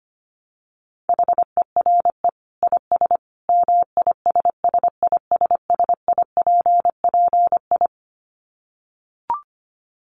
Morse code, heard as a keyed tone, about 25 wpm, 700 Hz. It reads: HERE IS MISSISSIPPI
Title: Dit Challenge 25wpm